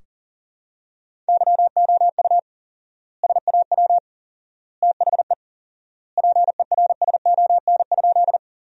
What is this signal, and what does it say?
Morse code, heard as a keyed tone, about 40 wpm, 700 Hz. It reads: YOU SAW THE PERSON?
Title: 4x Word-Spacing - Sentences from Top 400 Words - 40wpm